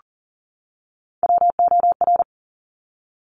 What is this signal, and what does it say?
Morse code, heard as a keyed tone, about 42 wpm, 700 Hz. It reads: WOR
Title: Top Three Letter Combinations in Words 40wpm